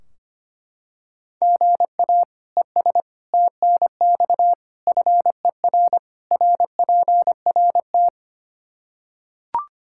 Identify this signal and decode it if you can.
Morse code, heard as a keyed tone, about 25 wpm, 700 Hz. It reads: GA ES TNX FER RPRT
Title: K1USN SST QSO Practice 25wpm